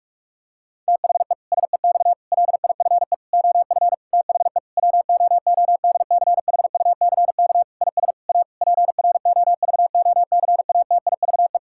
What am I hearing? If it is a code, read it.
Morse code, heard as a keyed tone, about 48 wpm, 700 Hz. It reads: THE SEX LIFE OF THE WOODCHUCK IS A PROVOCATIVE
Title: Pangram Challenge v2 - 45wpm